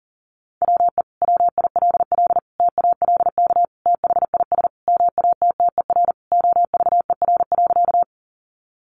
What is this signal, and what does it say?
Morse code, heard as a keyed tone, about 40 wpm, 700 Hz. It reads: WE WILL TALK THIS MATTER OVER.
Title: Sentences from Top 700 Words 40wpm